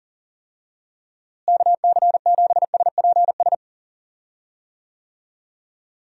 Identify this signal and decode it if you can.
Morse code, heard as a keyed tone, about 40 wpm, 700 Hz. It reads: KC7SPS
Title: USA Call Signs - 6 Characters - Repeated 3x - 40wpm